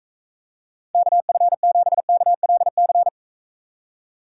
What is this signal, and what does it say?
Morse code, heard as a keyed tone, about 45 wpm, 700 Hz. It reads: KF7KLC